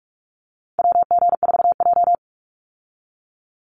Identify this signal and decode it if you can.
Morse code, heard as a keyed tone, about 44 wpm, 700 Hz. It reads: WG4J
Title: USA Call Signs 45wpm